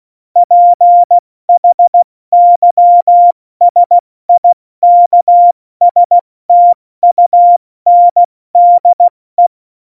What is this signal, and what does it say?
Morse code, heard as a keyed tone, about 16 wpm, 700 Hz. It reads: PHYSIKSTUNDE
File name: Morsekurs / Witze / Witz 02. April 2019, 70 F80